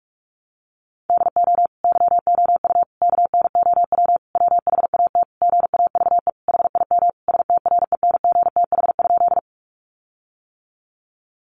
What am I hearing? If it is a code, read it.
Morse code, heard as a keyed tone, about 44 wpm, 700 Hz. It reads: DO YOU KNOW WHAT GAVE HIM STRENGTH?
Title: Sentences from Top 1300 Words 45wpm